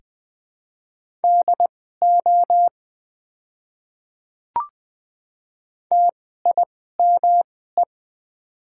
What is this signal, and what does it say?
Morse code, heard as a keyed tone, about 20 wpm, spaced as for 14 wpm, 690 Hz. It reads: DO TIME
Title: 2x Character Spacing - Top 100 Words - 20wpm